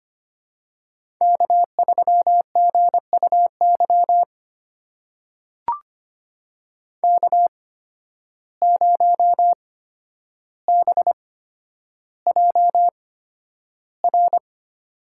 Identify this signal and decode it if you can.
Morse code, heard as a keyed tone, about 25 wpm, spaced as for 11 wpm, 700 Hz. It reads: K3GUY K0BJR
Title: USA Call Signs - 5 Characters - 8x Character Spacing - 25wpm